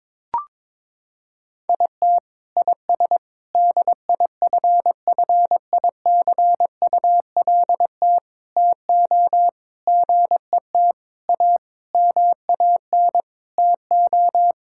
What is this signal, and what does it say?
Morse code, heard as a keyed tone, about 22 wpm, 700 Hz. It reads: IT IS DIFFICULT TO GET A MAN TO
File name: Sets of 9 Words - 22wpm